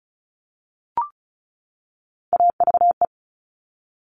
Morse code, heard as a keyed tone, about 36 wpm, 700 Hz. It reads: AVE
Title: Top Three Letter Combinations in Words 35wpm